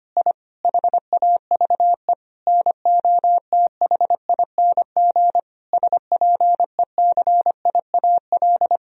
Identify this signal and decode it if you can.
Morse code, heard as a keyed tone, about 25 wpm, 700 Hz. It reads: I HAVE NOTHING SPECIAL
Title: Sentences from Top 1000 Words - Review Part 3 - 25wpm